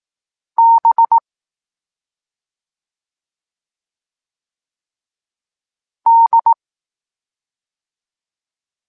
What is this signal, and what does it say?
Morse code, heard as a keyed tone, about 18 wpm, 920 Hz. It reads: B D
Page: Tekst 14 – doskonalenie odbioru poznanych liter